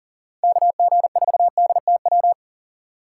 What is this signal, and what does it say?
Morse code, heard as a keyed tone, about 40 wpm, 700 Hz. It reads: KG4DTW
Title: USA Call Signs - 6 Characters - 2x Character Spacing - 40wpm